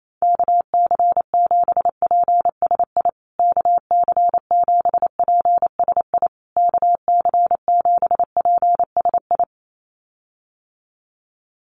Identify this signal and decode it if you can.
Morse code, heard as a keyed tone, about 28 wpm, 700 Hz. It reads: KC7PSI KC7PSI KC7PSI
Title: USA Call Signs Repeated 3x 28wpm